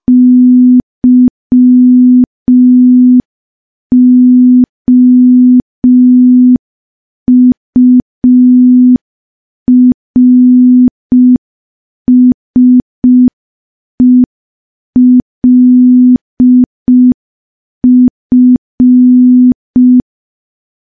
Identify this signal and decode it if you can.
Morse code, heard as a keyed tone, about 5 wpm, 250 Hz. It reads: YOURSELF
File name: morse / morse-training-05WPM